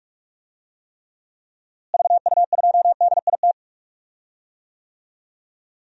Morse code, heard as a keyed tone, about 42 wpm, 700 Hz. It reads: VU2DIT
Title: DXCC-100 Call Signs - 6 Characters - Repeated 3x - 45wpm